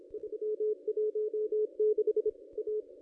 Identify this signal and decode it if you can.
Morse code, heard as a keyed tone, about 27 wpm, 410 Hz. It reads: 316 A
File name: contests / ss-cw-k5tr-op-n5rz -1400